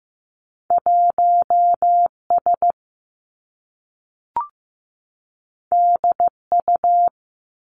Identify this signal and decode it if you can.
Morse code, heard as a keyed tone, about 15 wpm, 700 Hz. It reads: 1 S D U
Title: Two Challenging Letter-Number Combinations 15wpm